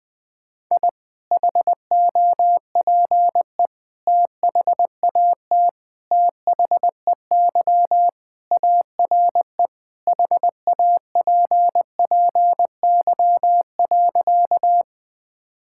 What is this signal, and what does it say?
Morse code, heard as a keyed tone, about 20 wpm, 700 Hz. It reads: I HOPE THAT THEY ARE HAPPY.
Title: Sentences with 3-Word Phrases - 20wpm